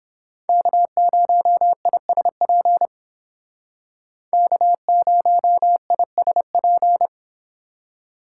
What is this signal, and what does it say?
Morse code, heard as a keyed tone, about 30 wpm, 700 Hz. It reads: K0ISP K0ISP